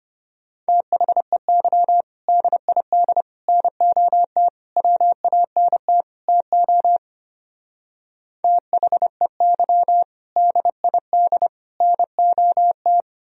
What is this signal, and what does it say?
Morse code, heard as a keyed tone, about 30 wpm, 700 Hz. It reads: THEY DID NOT WANT TO THEY DID NOT
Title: Speed-Racing - Sets of 5 Words 25wpm